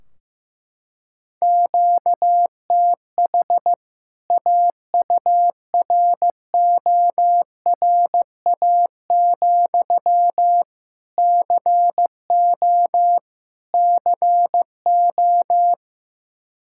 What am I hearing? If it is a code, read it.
Morse code, heard as a keyed tone, about 16 wpm, 700 Hz. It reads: QTH AURORA, CO CO
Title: K1USN SST QSO Practice 15wpm